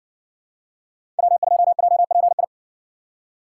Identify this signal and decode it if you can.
Morse code, heard as a keyed tone, about 56 wpm, 700 Hz. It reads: W2JPI